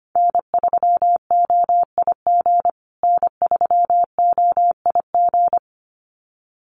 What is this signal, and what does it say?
Morse code, heard as a keyed tone, about 25 wpm, 700 Hz. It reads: N3OIG N3OIG
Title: USA Call Signs Repeated 2x 25wpm